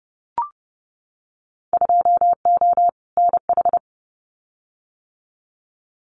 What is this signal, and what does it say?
Morse code, heard as a keyed tone, about 28 wpm, 700 Hz. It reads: 2O NH
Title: Winter Field Day Exchanges 30wpm